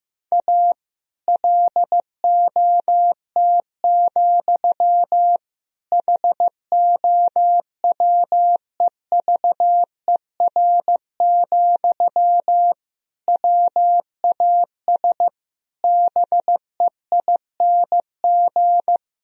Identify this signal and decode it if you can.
Morse code, heard as a keyed tone, about 15 wpm, 700 Hz. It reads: A LOT, HOWEVER, WAS BEING
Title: Sentences from Top 1000 Words - Review Part 3 - 15wpm